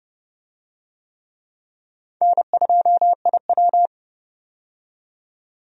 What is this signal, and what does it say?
Morse code, heard as a keyed tone, about 32 wpm, 700 Hz. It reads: N2IW
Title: USA Call Signs - 4 Characters - 30wpm